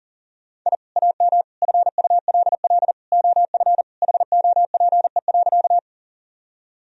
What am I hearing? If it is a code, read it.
Morse code, heard as a keyed tone, about 40 wpm, 700 Hz. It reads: I AM FULL OF HOPE.
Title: Sentences from Top 1000 Words - Review Part 3 - 40wpm